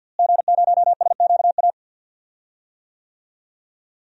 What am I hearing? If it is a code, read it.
Morse code, heard as a keyed tone, about 49 wpm, 700 Hz. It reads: G0SQA
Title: DXCC-100 Call Signs - 5 Characters - Repeated 2x - 50wpm